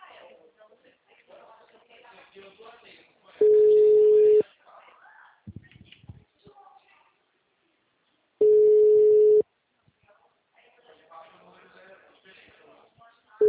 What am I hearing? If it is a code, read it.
Ringback tone (Europe, 425 Hz), with speech.